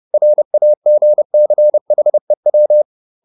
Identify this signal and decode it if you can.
Morse code, heard as a keyed tone, about 30 wpm, 600 Hz. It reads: RAGCHEW